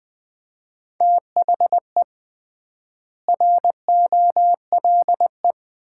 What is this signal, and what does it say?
Morse code, heard as a keyed tone, about 20 wpm, 700 Hz. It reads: THE ROLE
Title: 3x Word-Spacing - Sets of 3 Words - 20wpm